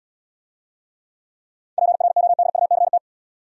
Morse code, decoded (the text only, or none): PRODUCT